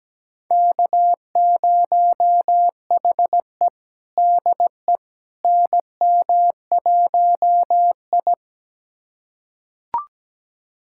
Morse code, heard as a keyed tone, about 17 wpm, 700 Hz. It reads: K0HE DE NM1I
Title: USA Callsign DE Callsign - 4 Characters - 17wpm